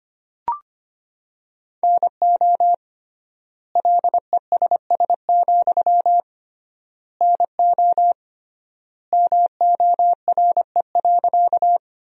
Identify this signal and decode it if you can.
Morse code, heard as a keyed tone, about 25 wpm, 700 Hz. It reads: NO LESS, NO MORE.
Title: 3x Word-Spacing - Sentences from Top 400 Words - 25wpm